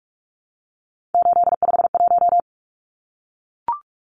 Morse code, heard as a keyed tone, about 44 wpm, 700 Hz. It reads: 851
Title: Numbers - 3-Digit 45wpm